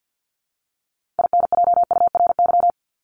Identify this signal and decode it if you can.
Morse code, heard as a keyed tone, about 50 wpm, 700 Hz. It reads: INJURY
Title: Top 1751-2000 Words 50wpm